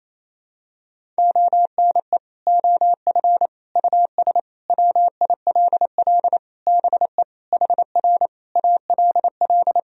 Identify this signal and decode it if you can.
Morse code, heard as a keyed tone, about 28 wpm, 700 Hz. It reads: ONE OF US WILL BE HR ALL
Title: Sentences from Top 300 Words and Abbreviations 28wpm